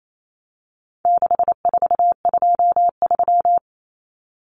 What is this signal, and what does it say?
Morse code, heard as a keyed tone, about 28 wpm, 700 Hz. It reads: 6423
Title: Numbers - 4-Digit 28wpm